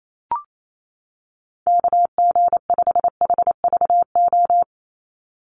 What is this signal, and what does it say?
Morse code, heard as a keyed tone, about 28 wpm, 700 Hz. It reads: KG5HVO
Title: Call Signs - 28wpm